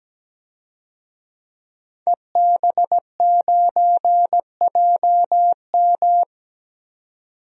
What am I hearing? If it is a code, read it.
Morse code, heard as a keyed tone, about 17 wpm, 700 Hz. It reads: EB9JM
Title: DXCC-100 Call Signs - 5 Characters - 17wpm